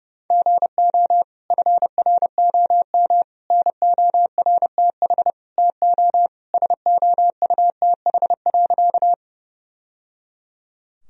Morse code, heard as a keyed tone, about 30 wpm, 700 Hz. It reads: GO FROM NORTH TO SOUTH.
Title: Sentences from Top 1000 Words - Review Part 2 - 30wpm